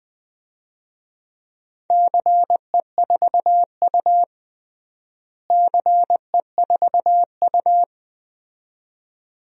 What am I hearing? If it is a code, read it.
Morse code, heard as a keyed tone, about 20 wpm, 700 Hz. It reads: CE4U CE4U
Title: DXCC-100 Call Signs - 4 Characters - Repeated 2x - 20wpm